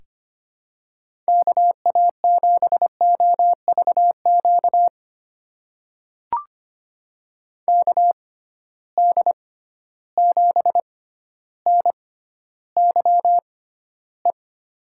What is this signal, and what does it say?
Morse code, heard as a keyed tone, about 25 wpm, spaced as for 13 wpm, 700 Hz. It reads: KA7OVQ KD7NYE
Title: USA Call Signs - 6 Characters - 6x Character Spacing - 25wpm